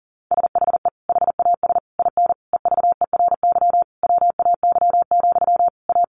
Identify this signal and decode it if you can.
Morse code, heard as a keyed tone, about 40 wpm, 700 Hz. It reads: SHE HAS IN EVERY WAY, A
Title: Sentences from Top 1300 Words 40wpm